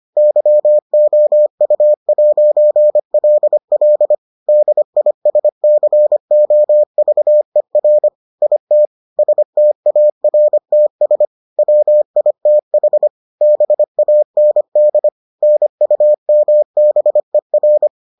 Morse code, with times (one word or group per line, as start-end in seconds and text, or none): 0.2-4.1 YOU'LL
4.5-8.1 DISCOVER
8.4-8.9 IT
9.2-11.3 STARTS
11.6-13.1 WITH
13.4-15.1 BAND
15.4-17.9 NUMBER